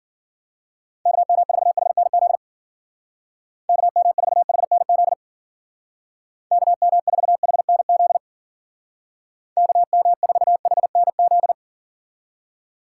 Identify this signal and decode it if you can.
Morse code, heard as a keyed tone, about 50 wpm, 700 Hz. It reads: KM4HNZ KM4HNZ KM4HNZ KM4HNZ
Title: Speed Racing - USA Call Signs 40wpm